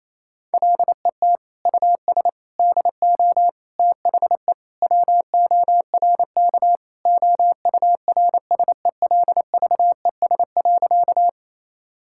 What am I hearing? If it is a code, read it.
Morse code, heard as a keyed tone, about 28 wpm, 700 Hz. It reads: LET US DO THE WORK OURSELVES.
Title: Sentences from Top 1600 Words 28wpm